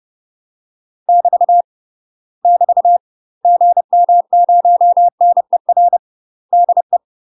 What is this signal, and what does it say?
Morse code, heard as a keyed tone, about 30 wpm, 700 Hz. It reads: <BT> <BT> GM0NER DE
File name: UK MASTER QSOs TRAINING/30WPM